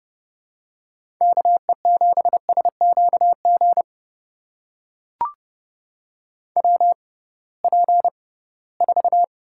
Morse code, heard as a keyed tone, about 30 wpm, spaced as for 19 wpm, 700 Hz. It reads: KE7SQG WP4
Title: USA Call Signs - 6 Characters - 6x Character Spacing - 30wpm